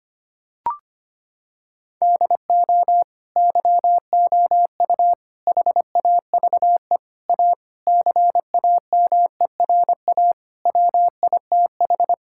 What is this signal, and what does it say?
Morse code, heard as a keyed tone, about 25 wpm, 700 Hz. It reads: DO YOU HAVE A CAMERA WITH
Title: Sentences from Top 1000 Words - Review Part 5 - 25wpm